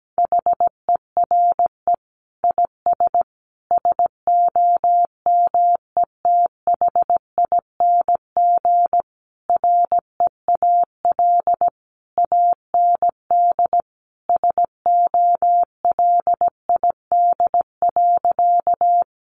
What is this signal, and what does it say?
Morse code, heard as a keyed tone, about 17 wpm, 700 Hz. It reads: HERE IS SOMETHING REAL AND SOLID.
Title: Sentences from Top 1500 Words 17wpm